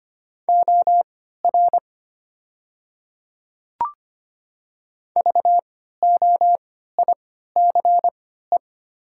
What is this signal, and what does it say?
Morse code, heard as a keyed tone, about 25 wpm, spaced as for 14 wpm, 690 Hz. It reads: OR VOICE
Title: 3x Character Spacing - Top 500 Words - 25wpm